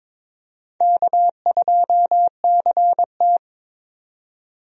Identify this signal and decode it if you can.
Morse code, heard as a keyed tone, about 22 wpm, 700 Hz. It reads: K2CT
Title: USA Call Signs - 4 Characters - 22wpm